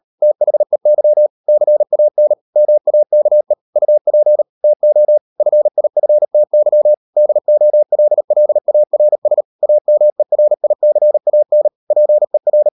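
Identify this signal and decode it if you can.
Morse code, heard as a keyed tone, about 38 wpm, 600 Hz. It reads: THEY CAN MAKE UP TO FIFTY DOLLARS AMERICAN PER
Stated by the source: Morse code practice at 38 WPM, sending Ken Follett's novel Never in English, from CW med Gnister.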